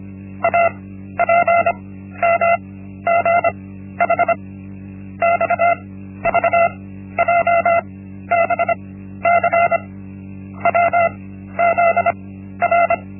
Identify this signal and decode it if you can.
Morse code, heard as a keyed tone, about 25 wpm, 680 Hz. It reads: A P M G H X V J B C W Z R
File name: Linksmakalnis-1